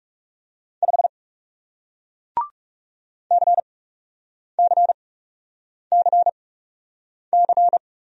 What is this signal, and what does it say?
Morse code, heard as a keyed tone, about 37 wpm, spaced as for 8 wpm, 700 Hz. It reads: 5 CCCC
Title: Speed-Racing Single Letter-Number 30wpm